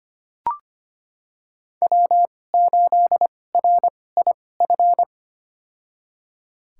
Morse code, heard as a keyed tone, about 25 wpm, spaced as for 18 wpm, 700 Hz. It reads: W8RIF
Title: USA Call Signs - 5 Characters - 2x Character Spacing - 25wpm